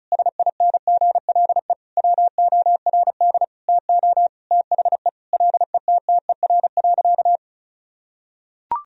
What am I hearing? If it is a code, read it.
Morse code, heard as a keyed tone, about 36 wpm, 700 Hz. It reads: SINGLE WORD TO THE LETTER.
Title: Sentences from Top 1000 Words - Review Part 1 - 35wpm